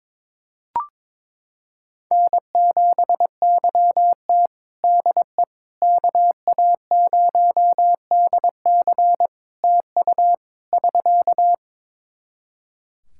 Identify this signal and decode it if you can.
Morse code, heard as a keyed tone, about 22 wpm, 700 Hz. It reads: N7YT DE KA0DC TU <SK>
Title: Practice QSO - Exchange 4 - Ending - 22wpm